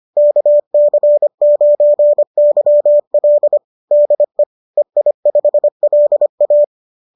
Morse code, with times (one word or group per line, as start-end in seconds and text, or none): 0.2-3.6 KC9YL
3.9-4.4 DE
4.8-6.6 EI5LA